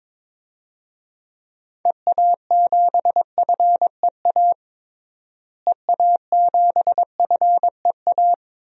Morse code, heard as a keyed tone, about 21 wpm, 700 Hz. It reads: EA7FEA EA7FEA
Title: DXCC-100 Call Signs - 6 Characters - Repeated 3x - 22wpm